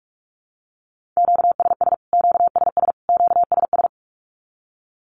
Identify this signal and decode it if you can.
Morse code, heard as a keyed tone, about 44 wpm, 700 Hz. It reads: QSS QSS QSS
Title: Q-Codes Repeated 3x 45wpm